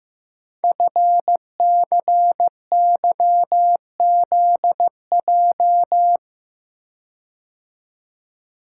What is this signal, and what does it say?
Morse code, heard as a keyed tone, about 15 wpm, 700 Hz. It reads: FCYZJ
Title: Code Groups - 5 Letters - 15wpm